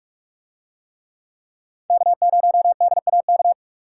Morse code, heard as a keyed tone, about 44 wpm, 700 Hz. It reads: K0DAK